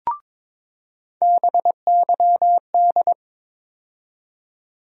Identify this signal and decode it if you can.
Morse code, heard as a keyed tone, about 22 wpm, 700 Hz. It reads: BYD
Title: USA Call Sign Postfixes - 3 Letters 22wpm